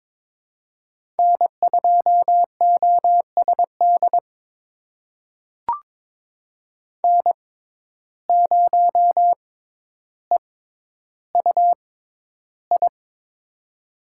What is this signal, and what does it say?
Morse code, heard as a keyed tone, about 21 wpm, spaced as for 11 wpm, 700 Hz. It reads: N2OSD N0EUI
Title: USA Call Signs - 5 Characters - 6x Character Spacing - 22wpm